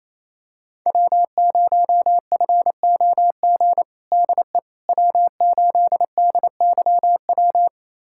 Morse code, heard as a keyed tone, about 28 wpm, 700 Hz. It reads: W0FOG DE W8DYW
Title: USA Callsign DE Callsign - 5 Characters - 28wpm